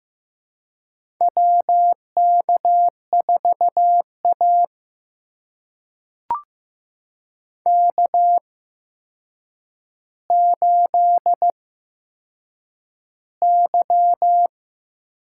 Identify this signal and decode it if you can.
Morse code, heard as a keyed tone, about 15 wpm, spaced as for 7 wpm, 700 Hz. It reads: WK4A K8Y
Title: USA Call Signs - 4 Characters - 8x Character Spacing - 15wpm